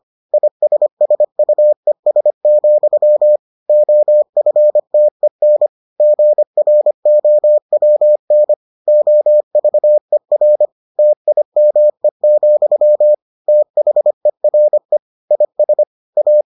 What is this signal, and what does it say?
Morse code, heard as a keyed tone, about 25 wpm, 600 Hz. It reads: ISSUES, OFTEN GROWN OVER TIME, THERE IS A